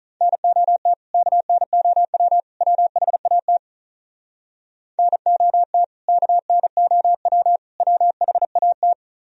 Morse code, heard as a keyed tone, about 43 wpm, 690 Hz. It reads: NOT KNOW WHAT NOT KNOW WHAT
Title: Speed-Racing - Sets of 3 Words 35wpm